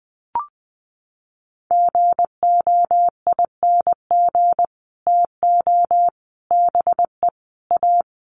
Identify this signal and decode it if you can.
Morse code, heard as a keyed tone, about 20 wpm, 700 Hz. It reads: GOING TO BE A
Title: Sets of 5 Words Encore 20wpm